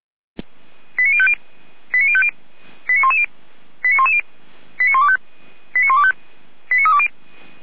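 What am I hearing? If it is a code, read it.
Five-tone selective call (ZVEI), heard as quick runs of five tones, tones of about 70 ms. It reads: 89050 89050 89110 89110 89125 89125 89320
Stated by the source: ZVEI five-tone selective call received on an SDR.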